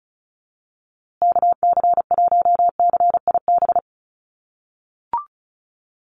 Morse code, heard as a keyed tone, about 34 wpm, 700 Hz. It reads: KC1CIB
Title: USA Call Signs Repeated 3x 35wpm